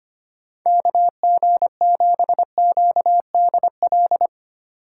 Morse code, heard as a keyed tone, about 25 wpm, 700 Hz. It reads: KG7QDL
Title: USA Call Signs - 6 Characters - 2x Character Spacing - 25wpm